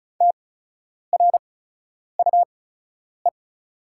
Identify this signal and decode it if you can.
Morse code, heard as a keyed tone, about 36 wpm, spaced as for 10 wpm, 700 Hz. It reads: TRUE
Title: 8x Character Spacing - Top 500 Words - 35wpm